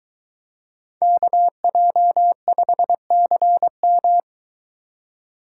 Morse code, heard as a keyed tone, about 23 wpm, 700 Hz. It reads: KJ5CM